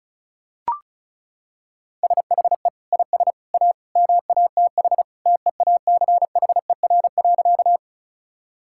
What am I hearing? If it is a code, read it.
Morse code, heard as a keyed tone, about 36 wpm, 700 Hz. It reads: SHE IS A MATH TEACHER.
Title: Sentences from Top 1000 Words - Review Part 6 - 35wpm